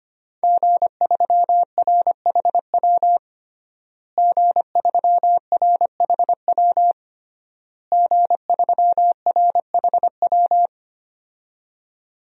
Morse code, heard as a keyed tone, about 25 wpm, 700 Hz. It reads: G3RHW G3RHW G3RHW
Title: DXCC-100 Call Signs - 5 Characters - Repeated 3x - 25wpm